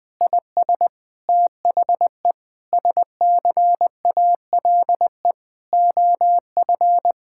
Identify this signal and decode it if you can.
Morse code, heard as a keyed tone, about 20 wpm, 700 Hz. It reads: IS THE SCALE OF
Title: Sentences from Top 1000 Words - Review Part 5 - 20wpm